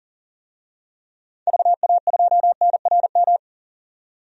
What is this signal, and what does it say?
Morse code, heard as a keyed tone, about 40 wpm, 700 Hz. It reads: VA2NRM
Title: DXCC-100 Call Signs - 6 Characters - 40wpm